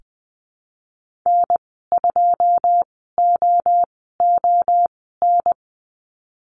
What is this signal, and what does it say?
Morse code, heard as a keyed tone, about 20 wpm, spaced as for 15 wpm, 700 Hz. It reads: N2OON
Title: 122 Lesson 20g - Number 3 - Call Signs 20wpm